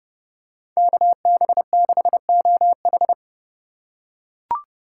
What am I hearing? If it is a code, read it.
Morse code, heard as a keyed tone, about 30 wpm, 700 Hz. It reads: KB6OH